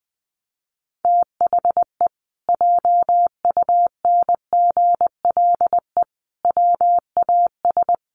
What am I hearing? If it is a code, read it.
Morse code, heard as a keyed tone, about 20 wpm, 700 Hz. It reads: THE JUNGLE WAS
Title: A Sound Of Thunder 20wpm